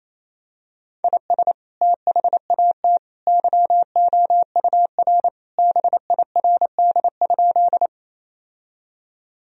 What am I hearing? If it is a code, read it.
Morse code, heard as a keyed tone, about 27 wpm, 700 Hz. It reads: IS THAT YOUR BIRD?
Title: Sentences from Top 1000 Words - Review Part 2 - 28wpm